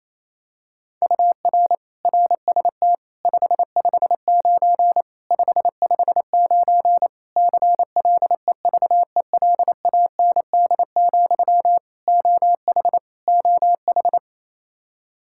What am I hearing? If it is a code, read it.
Morse code, heard as a keyed tone, about 28 wpm, 700 Hz. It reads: UR RST 559 559 CLEVELAND, OH OH
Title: Practice QSO - Exchange 1 - Beginning - 28wpm